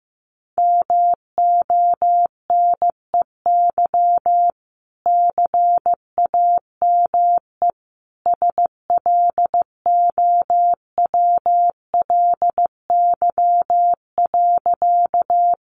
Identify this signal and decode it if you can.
Morse code, heard as a keyed tone, about 15 wpm, 700 Hz. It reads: MONEY CAME SLOWLY.
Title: Sentences from Top 1100 Words 15wpm